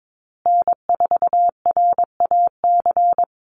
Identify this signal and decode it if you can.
Morse code, heard as a keyed tone, about 22 wpm, 700 Hz. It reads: N4RAC
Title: USA Call Signs Repeated 2x 22wpm